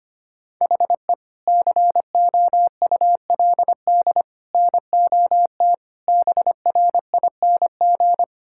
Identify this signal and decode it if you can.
Morse code, heard as a keyed tone, about 25 wpm, 700 Hz. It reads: HE COULD NOT BRING
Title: Sentences from Top 1600 Words 25wpm